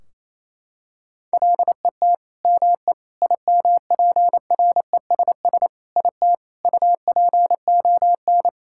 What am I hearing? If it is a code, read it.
Morse code, heard as a keyed tone, about 28 wpm, 700 Hz. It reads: LET ME IMPRESS IT UPON
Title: Sentences from Top 1600 Words 28wpm